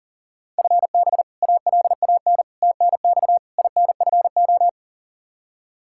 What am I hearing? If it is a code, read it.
Morse code, heard as a keyed tone, about 40 wpm, 700 Hz. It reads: FB ALAN TNX INFO
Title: Practice QSO - Exchange 3 - More Info - 40wpm